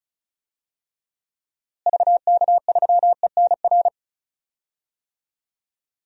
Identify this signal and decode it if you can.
Morse code, heard as a keyed tone, about 36 wpm, 700 Hz. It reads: VK3ENR
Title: DXCC-100 Call Signs - 6 Characters - Repeated 3x - 35wpm